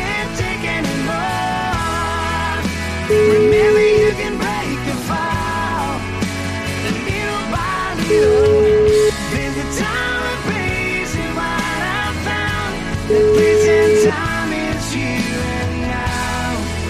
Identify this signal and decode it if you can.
Ringback tone (Europe, 425 Hz), with noise.